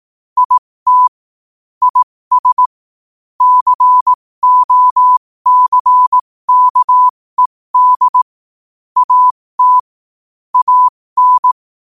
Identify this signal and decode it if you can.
Morse code, heard as a keyed tone, about 18 wpm, spaced as for 15 wpm, 1000 Hz. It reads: IT IS COCKED AT AN
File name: qotdfiles / 15wpm